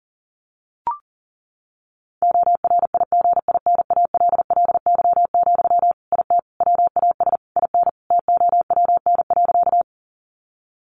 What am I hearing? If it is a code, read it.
Morse code, heard as a keyed tone, about 40 wpm, 700 Hz. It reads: ORIGINALLY, IT WAS IN TOWN.
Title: Sentences from Top 1500 Words 40wpm